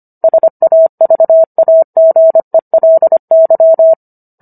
Morse code, heard as a keyed tone, about 25 wpm, 650 Hz. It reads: SAVAGELY